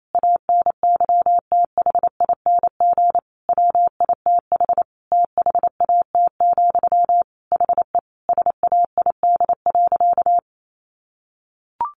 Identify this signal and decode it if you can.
Morse code, heard as a keyed tone, about 28 wpm, 700 Hz. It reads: ANYTHING WITH THAT, HE SAID.
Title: Sentences from Top 1000 Words 28wpm